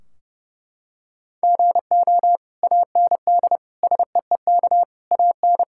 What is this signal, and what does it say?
Morse code, heard as a keyed tone, about 30 wpm, 700 Hz. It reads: GO AND SEEK AN
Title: Sentences from Top 1600 Words 30wpm